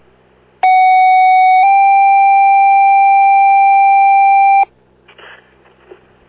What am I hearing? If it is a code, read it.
Two-tone page, heard as a one-second tone, then a three-second tone, about 750 then 790 Hz. It is heard from 0.6 to 4.6 s.